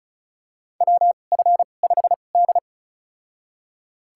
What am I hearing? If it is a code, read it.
Morse code, heard as a keyed tone, about 36 wpm, spaced as for 25 wpm, 700 Hz. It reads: WF5D
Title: USA Call Signs - 4 Characters - 2x Character Spacing - 35wpm